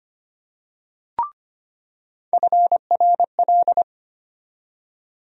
Morse code, heard as a keyed tone, about 25 wpm, 700 Hz. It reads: FRL
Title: USA Call Sign Postfixes - 3 Letters 25wpm